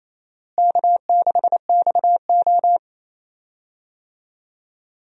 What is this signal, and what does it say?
Morse code, heard as a keyed tone, about 28 wpm, 700 Hz. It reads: K6XO
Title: USA Call Signs - 4 Characters - Repeated 3x - 28wpm